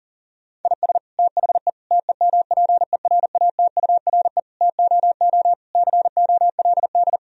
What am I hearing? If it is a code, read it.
Morse code, heard as a keyed tone, about 37 wpm, 700 Hz. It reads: IS THE TEMPERATURE TOO COLD
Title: Sentences from Top 1000 Words - Review Part 4 - 40wpm